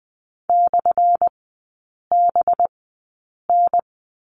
Morse code, heard as a keyed tone, about 20 wpm, spaced as for 8 wpm, 700 Hz. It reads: /BN
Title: 218 Lesson 35e - Number 0 - Three Characters 20wpm